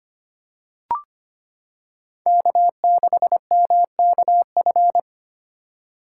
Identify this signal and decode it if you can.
Morse code, heard as a keyed tone, about 25 wpm, 700 Hz. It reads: K6MKF
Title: K1USN SST QSO Practice - 25wpm